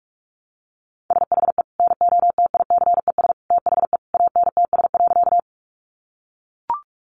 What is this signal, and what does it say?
Morse code, heard as a keyed tone, about 44 wpm, 700 Hz. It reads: SHE NOTICES THE ANTS.
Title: Easy Phrases 45wpm